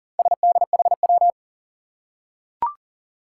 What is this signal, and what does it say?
Morse code, heard as a keyed tone, about 40 wpm, 700 Hz. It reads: SDHW
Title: Four Challenging Letter-Number Combinations 40wpm